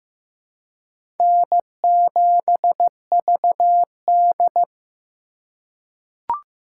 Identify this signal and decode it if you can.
Morse code, heard as a keyed tone, about 15 wpm, 700 Hz. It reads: N7VD
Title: USA Call Signs - 4x Character Spacing - 15wpm